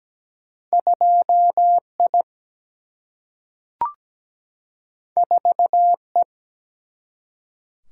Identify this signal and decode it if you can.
Morse code, heard as a keyed tone, about 17 wpm, 700 Hz. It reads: 2 I 4 E
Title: Single-Letter Single-Number 17wpm